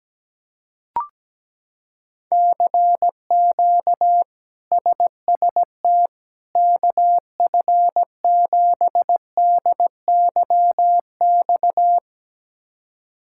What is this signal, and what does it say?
Morse code, heard as a keyed tone, about 17 wpm, 700 Hz. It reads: CQ SST KF7DYX
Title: K1USN SST QSO Practice - 17wpm